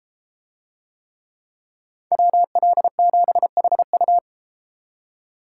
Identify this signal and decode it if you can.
Morse code, heard as a keyed tone, about 32 wpm, 700 Hz. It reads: WL7HU